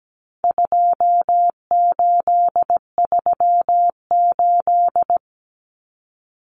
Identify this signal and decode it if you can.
Morse code, heard as a keyed tone, about 17 wpm, 700 Hz. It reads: 2838
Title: Common Cut Numbers - 4-Digit 17wpm